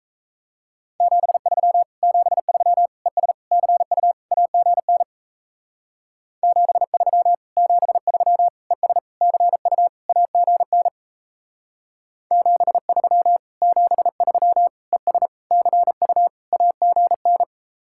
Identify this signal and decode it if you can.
Morse code, heard as a keyed tone, about 37 wpm, 700 Hz. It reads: 73 73 ES CU AGN 73 73 ES CU AGN 73 73 ES CU AGN